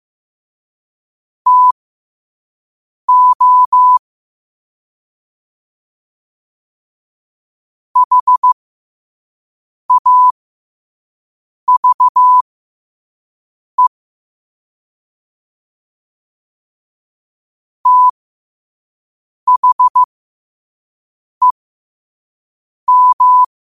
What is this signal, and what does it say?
Morse code, heard as a keyed tone, about 15 wpm, spaced as for 5 wpm, 1000 Hz. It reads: TO HAVE THEM
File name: qotdfiles / 5wpm